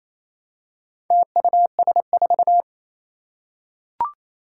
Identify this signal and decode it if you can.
Morse code, heard as a keyed tone, about 27 wpm, 700 Hz. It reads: TUS4
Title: Four Challenging Letter-Number Combinations 28wpm